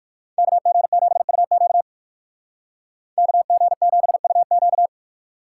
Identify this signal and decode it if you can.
Morse code, heard as a keyed tone, about 60 wpm, 700 Hz. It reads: KG7UQ KG7UQ